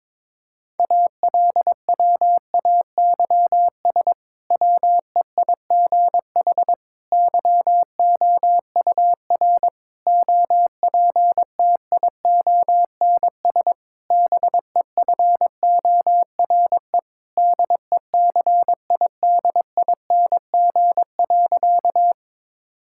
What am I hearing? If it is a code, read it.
Morse code, heard as a keyed tone, about 22 wpm, 700 Hz. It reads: ALWAYS WEIGH YOUR OPTIONS BEFORE DECIDING.
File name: Difficult Sentences with Hints - 22wpm